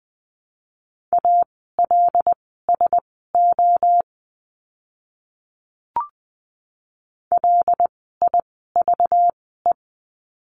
Morse code, heard as a keyed tone, about 20 wpm, spaced as for 14 wpm, 700 Hz. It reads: ALSO LIVE
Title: 192 Lesson 31f - Letter v - Words 20wpm